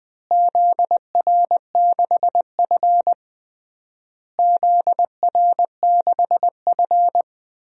Morse code, heard as a keyed tone, about 20 wpm, 700 Hz. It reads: ZR6F ZR6F
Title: DXCC-100 Call Signs - 4 Characters - Repeated 3x - 20wpm